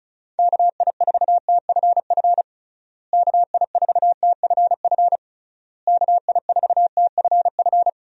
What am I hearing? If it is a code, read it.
Morse code, heard as a keyed tone, about 34 wpm, 700 Hz. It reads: KI4TFF KI4TFF KI4TFF
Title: USA Call Signs - 6 Characters - Repeated 3x - 35wpm